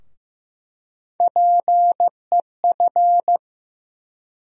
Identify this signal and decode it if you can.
Morse code, heard as a keyed tone, about 15 wpm, 700 Hz. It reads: PEF
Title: USA Call Sign Postfixes - 3 Letters 15wpm